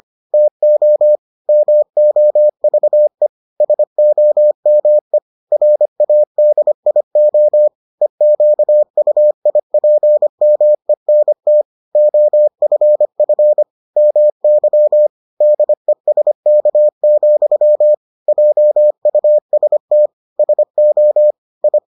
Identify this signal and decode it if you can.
Morse code, heard as a keyed tone, about 25 wpm, 600 Hz. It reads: TO MOVE SOME RADIO EQUIPMENT OFF MY DESK, JUST SO I